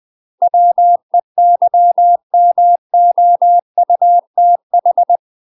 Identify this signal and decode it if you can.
Morse code, heard as a keyed tone, about 20 wpm, 700 Hz. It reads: WEYMOUTH
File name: UK MASTER QSOs TRAINING / 20WPM / UK MASTER QSOs 0011